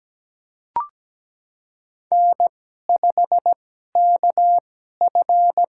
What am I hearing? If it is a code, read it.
Morse code, heard as a keyed tone, about 17 wpm, spaced as for 12 wpm, 700 Hz. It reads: N5KF
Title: USA Call Signs - 4 Characters - 2x Character Spacing - 17wpm